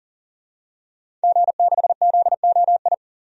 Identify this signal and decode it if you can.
Morse code, heard as a keyed tone, about 40 wpm, 700 Hz. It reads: G6ZOI
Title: DXCC-100 Call Signs - 5 Characters - 40wpm